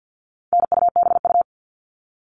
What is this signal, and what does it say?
Morse code, heard as a keyed tone, about 50 wpm, 700 Hz. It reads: NUBU
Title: All Cut Numbers - 4-Digit 50wpm